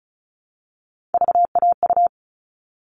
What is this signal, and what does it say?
Morse code, heard as a keyed tone, about 34 wpm, 700 Hz. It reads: VAU
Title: All Cut Numbers - 3-Digit 35wpm